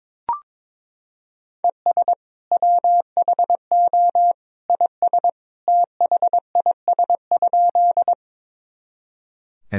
Morse code, heard as a keyed tone, about 22 wpm, 700 Hz. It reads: ES WHO IS THIS?
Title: Sentences from Top 100 Words and Abbreviations 22wpm